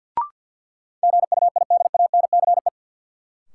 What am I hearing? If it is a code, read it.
Morse code, heard as a keyed tone, about 53 wpm, 700 Hz. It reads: GUIDANCE
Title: Sentences from Top 1600 Words - New Word Drill - 50wpm